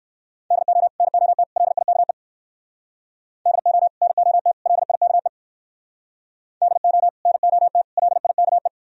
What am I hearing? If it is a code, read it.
Morse code, heard as a keyed tone, about 54 wpm, 700 Hz. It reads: DO NOT LIKE DO NOT LIKE DO NOT LIKE
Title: Speed-Racing - Sets of 3 Words 45wpm